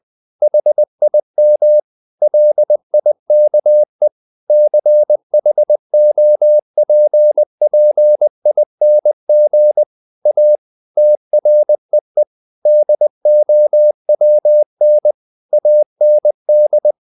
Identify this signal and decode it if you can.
Morse code, heard as a keyed tone, about 20 wpm, 600 Hz. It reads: HIM LIKE CHOPPING A TREE DOWN AND
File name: the old man and the sea / the old man and the sea part 1 20wpm